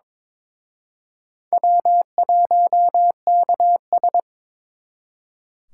Morse code, heard as a keyed tone, about 26 wpm, 700 Hz. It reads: W1KS
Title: Speed Racing - USA 1x2 Call Signs 22wpm